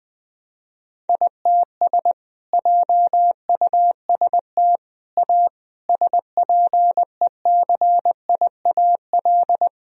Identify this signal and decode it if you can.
Morse code, heard as a keyed tone, about 20 wpm, 700 Hz. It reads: ITS JUST A SPECIAL
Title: Sentences from Top 1000 Words - Review Part 4 - 20wpm